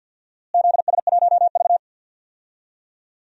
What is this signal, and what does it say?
Morse code, heard as a keyed tone, about 51 wpm, 700 Hz. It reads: ZS1V
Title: DXCC-100 Call Signs - 4 Characters - Repeated 3x - 50wpm